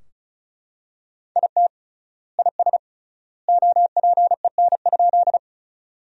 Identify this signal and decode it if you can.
Morse code, heard as a keyed tone, about 36 wpm, 700 Hz. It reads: IT IS OPEN?
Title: 3x Word-Spacing - Sentences from Top 300 Words - 35wpm